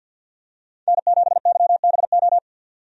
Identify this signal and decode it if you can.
Morse code, heard as a keyed tone, about 53 wpm, 700 Hz. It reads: N7YBO
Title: USA Call Signs - 5 Characters - 50wpm